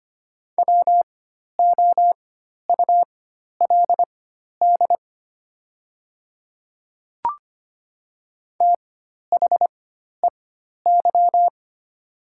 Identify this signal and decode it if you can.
Morse code, heard as a keyed tone, about 25 wpm, spaced as for 12 wpm, 700 Hz. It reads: WOULD THEY
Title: 4x Character Spacing - Top 200 Words - 25wpm